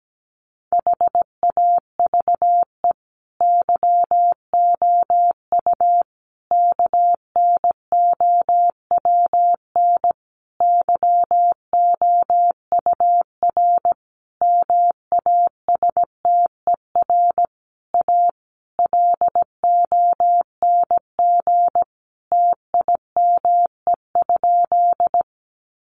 Morse code, heard as a keyed tone, about 17 wpm, 700 Hz. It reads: HAVE YOU KNOWN YOUR MASTER A LONG TIME?
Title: Sentences from Top 1300 Words 17wpm